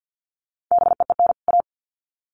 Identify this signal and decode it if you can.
Morse code, heard as a keyed tone, about 50 wpm, 700 Hz. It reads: BEEN A
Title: Sets of 2 Words 50wpm